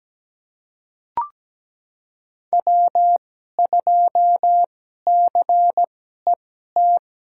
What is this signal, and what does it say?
Morse code, heard as a keyed tone, about 17 wpm, spaced as for 12 wpm, 700 Hz. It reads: W2CET
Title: USA Call Signs - 5 Characters - 2x Character Spacing - 17wpm